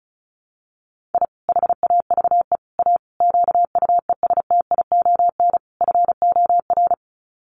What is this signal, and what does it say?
Morse code, heard as a keyed tone, about 34 wpm, 700 Hz. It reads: I HAVE A QUESTION FOR
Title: Sets of 5 Words Encore2 35wpm